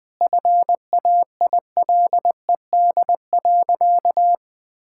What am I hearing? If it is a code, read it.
Morse code, heard as a keyed tone, about 21 wpm, 700 Hz. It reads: FAILED.
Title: Sentences with 9-Word Phrases - Spoken Beginning - 20wpm